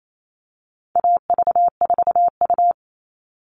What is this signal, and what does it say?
Morse code, heard as a keyed tone, about 28 wpm, 700 Hz. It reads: AV4U
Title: All Cut Numbers - 4-Digit 28wpm